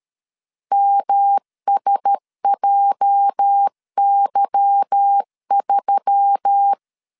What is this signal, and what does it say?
Morse code, heard as a keyed tone, about 13 wpm, 800 Hz. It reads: MSJY3
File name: LEZIONI / 8 YJ45 / 8 YJ 12wpm 2235